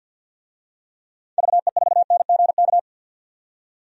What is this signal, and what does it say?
Morse code, heard as a keyed tone, about 53 wpm, 700 Hz. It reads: VE4NGK
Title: DXCC-100 Call Signs - 6 Characters - 50wpm